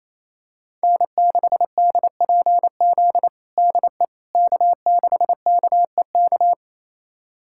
Morse code, heard as a keyed tone, about 28 wpm, 700 Hz. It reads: N6DPZ DE K6KEK
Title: USA Callsign DE Callsign - 5 Characters - 28wpm